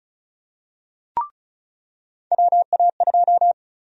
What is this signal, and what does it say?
Morse code, heard as a keyed tone, about 36 wpm, 700 Hz. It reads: WA2
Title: USA Call Sign Prefixes - 35wpm